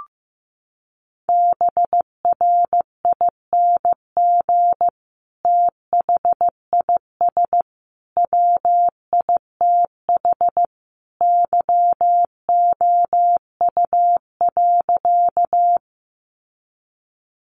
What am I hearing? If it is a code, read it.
Morse code, heard as a keyed tone, about 14 wpm, 700 Hz. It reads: BRING THIS WITH YOU.
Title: Sentences from Top 500 Words 15wpm